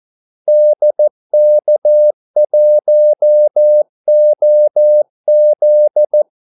Morse code, heard as a keyed tone, about 14 wpm, 600 Hz. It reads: DK1OZ
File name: Callsigns-14wpm